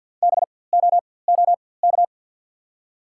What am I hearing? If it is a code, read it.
Morse code, heard as a keyed tone, about 54 wpm, spaced as for 24 wpm, 690 Hz. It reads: BOOK
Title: 4x Character Spacing - Top 300 Words - 50wpm